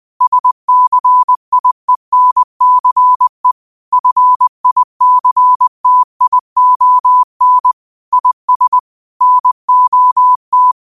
Morse code, heard as a keyed tone, about 20 wpm, 1000 Hz. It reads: SCIENCE FICTION IS NOT
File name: qotdfiles / 20wpm